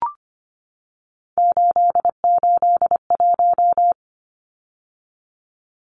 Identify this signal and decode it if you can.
Morse code, heard as a keyed tone, about 25 wpm, 700 Hz. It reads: 881
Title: Numbers - 3-Digit 25wpm